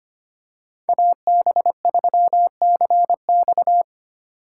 Morse code, heard as a keyed tone, about 25 wpm, 700 Hz. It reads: AB3CX
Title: USA Call Signs - 5 Characters - 2x Character Spacing - 25wpm